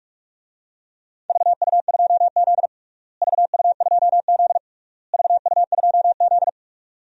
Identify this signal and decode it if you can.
Morse code, heard as a keyed tone, about 41 wpm, 700 Hz. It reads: VU2Z VU2Z VU2Z
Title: DXCC-100 Call Signs - 4 Characters - Repeated 3x - 45wpm